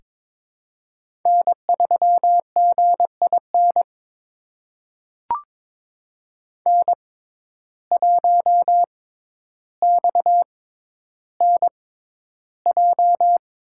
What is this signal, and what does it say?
Morse code, heard as a keyed tone, about 22 wpm, spaced as for 11 wpm, 700 Hz. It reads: N3GIN N1XNJ
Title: USA Call Signs - 5 Characters - 6x Character Spacing - 22wpm